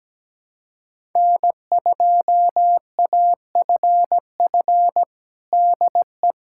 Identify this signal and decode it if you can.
Morse code, heard as a keyed tone, about 17 wpm, 700 Hz. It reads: N2AFF DE
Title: USA Callsign DE Callsign - 5 Characters - 17wpm